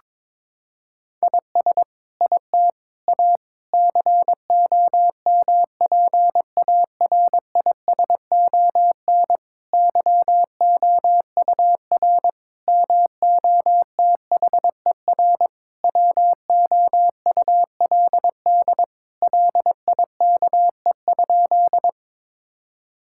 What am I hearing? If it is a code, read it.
Morse code, heard as a keyed tone, about 22 wpm, 700 Hz. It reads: IS IT A COMPARISON YOUR MOTHER WOULD LIKE?
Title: Sentences from Top 1600 Words 22wpm